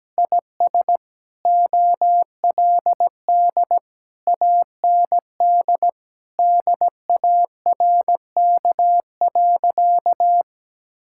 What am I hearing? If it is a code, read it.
Morse code, heard as a keyed tone, about 17 wpm, 700 Hz. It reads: IS OLD AND DARK.
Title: Sentences from Top 1000 Words - Review Part 5 - 17wpm